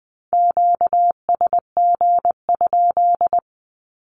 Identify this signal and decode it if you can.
Morse code, heard as a keyed tone, about 20 wpm, 700 Hz. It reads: QSG?
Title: Q-Codes 20wpm